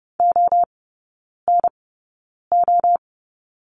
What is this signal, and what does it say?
Morse code, heard as a keyed tone, about 30 wpm, spaced as for 9 wpm, 700 Hz. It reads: ONO
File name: 081 Lesson 14e - Number 2 - Three Characters 30wpm